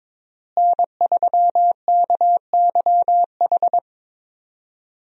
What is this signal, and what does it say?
Morse code, heard as a keyed tone, about 22 wpm, 700 Hz. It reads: N3KYH
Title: USA Call Signs - 5 Characters - Repeated 2x - 22wpm